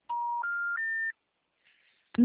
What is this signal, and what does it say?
Special information tone.